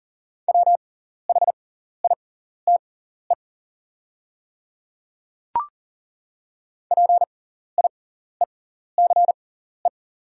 Morse code, heard as a keyed tone, about 40 wpm, spaced as for 14 wpm, 700 Hz. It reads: WHITE PIECE
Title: 6x Character Spacing - Top 500 Words - 40wpm